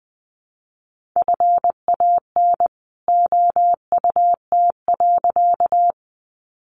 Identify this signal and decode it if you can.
Morse code, heard as a keyed tone, about 19 wpm, 700 Hz. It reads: FAN OUT.
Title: Sentences from Top 1300 Words 20wpm